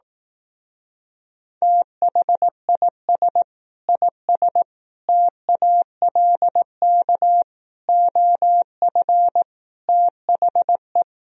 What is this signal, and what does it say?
Morse code, heard as a keyed tone, about 18 wpm, 700 Hz. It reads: THIS IS TALK OF THE
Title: Speed-Racing - Sets of 5 Words 15wpm